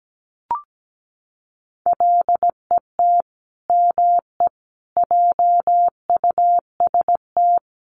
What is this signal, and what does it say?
Morse code, heard as a keyed tone, about 17 wpm, 700 Hz. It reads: LET ME JUST
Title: Sets of 5 Words Encore2 17wpm